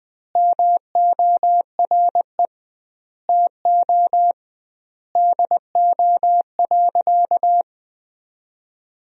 Morse code, heard as a keyed tone, about 20 wpm, 700 Hz. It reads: MORE TO DO.
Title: 2x Word-Spacing - Sentences from Top 100 Words - 20wpm